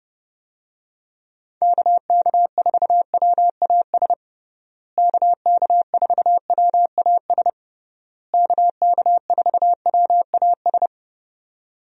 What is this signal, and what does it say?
Morse code, heard as a keyed tone, about 32 wpm, 700 Hz. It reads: KK4WAS KK4WAS KK4WAS
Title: USA Call Signs - 6 Characters - Repeated 3x - 30wpm